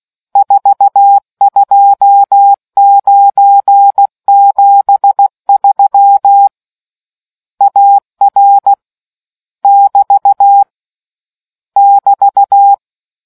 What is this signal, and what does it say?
Morse code, heard as a keyed tone, about 16 wpm, 800 Hz. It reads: 42973 AR <BT> <BT>